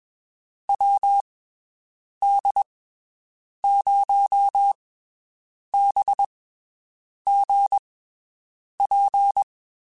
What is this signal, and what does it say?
Morse code, heard as a keyed tone, about 21 wpm, spaced as for 7 wpm, 800 Hz. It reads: WD0BGP
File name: (spaced) / QSOs Spaced-mp3s / WD0BGP 1 @20 sp